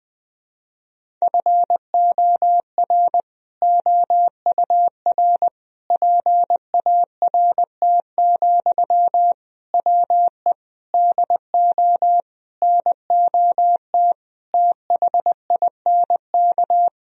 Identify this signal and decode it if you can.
Morse code, heard as a keyed tone, about 21 wpm, 700 Hz. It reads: FOR OUR PART, WE DO NOT THINK
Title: Sentences from Top 1000 Words - Review Part 1 - 20wpm